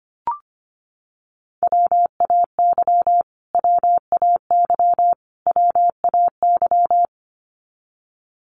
Morse code, heard as a keyed tone, about 25 wpm, 700 Hz. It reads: WAY WAY WAY
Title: Top 500 Words Repeated 25wpm